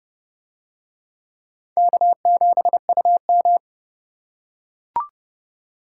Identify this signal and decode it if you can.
Morse code, heard as a keyed tone, about 31 wpm, 700 Hz. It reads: K7UM